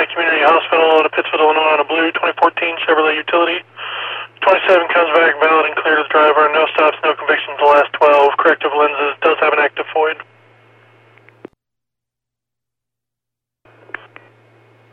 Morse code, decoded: none

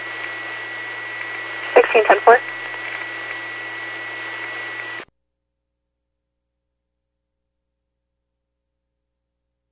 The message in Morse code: none

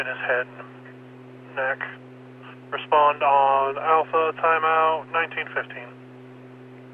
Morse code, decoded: none

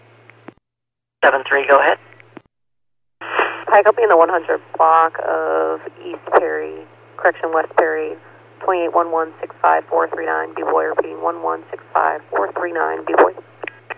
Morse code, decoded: none

